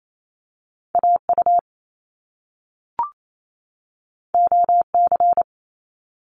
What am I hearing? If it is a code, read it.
Morse code, heard as a keyed tone, about 28 wpm, 700 Hz. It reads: AU OC
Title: Top Two Letter Combinations in Words 28wpm